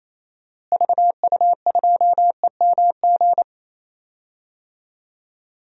Morse code, heard as a keyed tone, about 28 wpm, 700 Hz. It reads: VU2EMG
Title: DXCC-100 Call Signs - 6 Characters - Repeated 2x - 28wpm